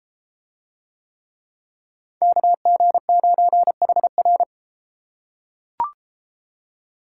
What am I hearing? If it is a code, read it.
Morse code, heard as a keyed tone, about 34 wpm, 700 Hz. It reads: KG9HR